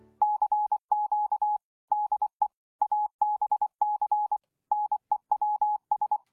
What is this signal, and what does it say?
Morse code, heard as a keyed tone, about 24 wpm, 840 Hz. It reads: CQ DE ABC NEWS